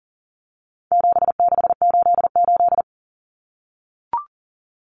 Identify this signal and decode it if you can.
Morse code, heard as a keyed tone, about 40 wpm, 700 Hz. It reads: 7688
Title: Common Cut Numbers - 4-Digit 40wpm